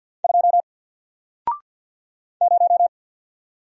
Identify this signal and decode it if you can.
Morse code, heard as a keyed tone, about 50 wpm, 700 Hz. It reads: 2 0
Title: Single Letter-Number 50wpm